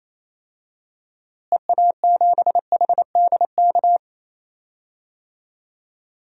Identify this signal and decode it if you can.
Morse code, heard as a keyed tone, about 28 wpm, 700 Hz. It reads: EA7HDK
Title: DXCC-100 Call Signs - 6 Characters - Repeated 3x - 28wpm